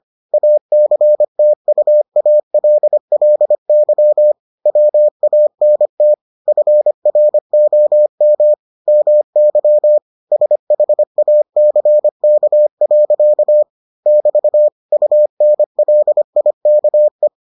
Morse code, heard as a keyed tone, about 25 wpm, 600 Hz. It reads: ACTUALLY WANT FROM MY SHACK. <BT> UNLIKE